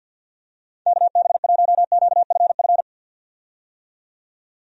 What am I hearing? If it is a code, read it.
Morse code, heard as a keyed tone, about 52 wpm, 700 Hz. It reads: KB1QFF
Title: USA Call Signs - 6 Characters - 50wpm